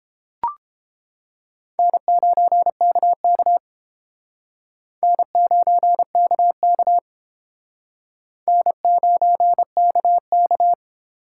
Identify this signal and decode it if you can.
Morse code, heard as a keyed tone, about 29 wpm, 700 Hz. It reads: N9KK N9KK N9KK